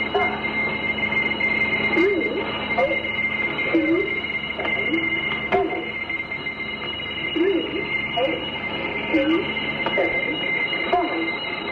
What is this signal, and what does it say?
Dial-up modem handshake, with speech.